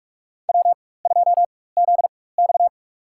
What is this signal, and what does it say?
Morse code, heard as a keyed tone, about 46 wpm, spaced as for 21 wpm, 700 Hz. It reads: W2ZX
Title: USA Call Signs - 4 Characters - 4x Character Spacing - 45wpm